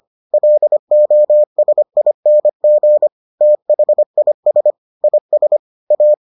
Morse code, heard as a keyed tone, about 25 wpm, 600 Hz. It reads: LOSING THIS IS A